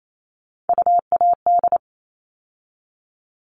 Morse code, heard as a keyed tone, about 28 wpm, 700 Hz. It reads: UAD
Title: All Cut Numbers - 3-Digit 28wpm